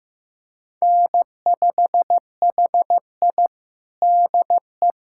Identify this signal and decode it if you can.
Morse code, heard as a keyed tone, about 15 wpm, 700 Hz. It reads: N5HI DE
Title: USA Callsign DE Callsign - 4 Characters - 15wpm